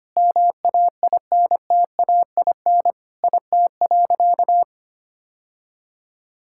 Morse code, heard as a keyed tone, about 25 wpm, 700 Hz. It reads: MAINTAIN IT.
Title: Sentences from Top 1000 Words - Review Part 5 - 25wpm